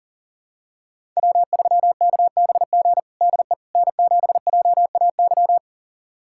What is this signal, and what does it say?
Morse code, heard as a keyed tone, about 40 wpm, 700 Hz. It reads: W3KBG DE N7JAY
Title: USA Callsign DE Callsign - 5 Characters - 40wpm